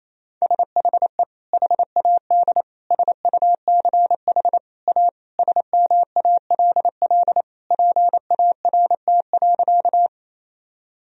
Morse code, heard as a keyed tone, about 27 wpm, 700 Hz. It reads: SHE HAD SUCH A SMALL PART.
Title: Sentences from Top 1000 Words - Review Part 1 - 28wpm